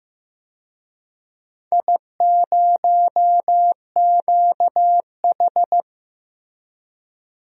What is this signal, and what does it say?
Morse code, heard as a keyed tone, about 15 wpm, 700 Hz. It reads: I0QH
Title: DXCC-100 Call Signs - 4 Characters - Repeated 3x - 15wpm